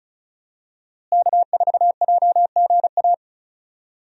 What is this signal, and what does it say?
Morse code, heard as a keyed tone, about 36 wpm, 700 Hz. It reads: K4JGA
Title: USA Call Signs - 5 Characters - Repeated 3x - 35wpm